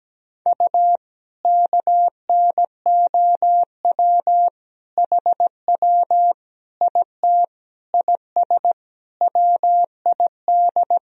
Morse code, heard as a keyed tone, about 17 wpm, 700 Hz. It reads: U KNOW HW IT IS WID
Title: Sentences from Top 100 Words and Abbreviations 17wpm